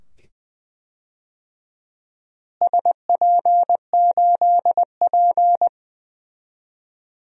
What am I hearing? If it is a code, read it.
Morse code, heard as a keyed tone, about 20 wpm, 700 Hz. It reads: SP8P
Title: DXCC-100 Call Signs - 4 Characters - Repeated 2x - 20wpm